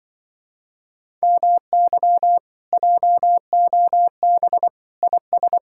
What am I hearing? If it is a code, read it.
Morse code, heard as a keyed tone, about 24 wpm, 700 Hz. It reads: MY JOB IS
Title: Speed-Racing Common QSO Phrases - 20wpm